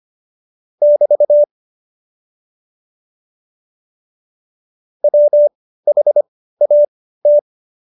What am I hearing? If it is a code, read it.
Morse code, heard as a keyed tone, about 25 wpm, spaced as for 15 wpm, 590 Hz. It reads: <BT> WHAT
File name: Jokes-25-char-15-eff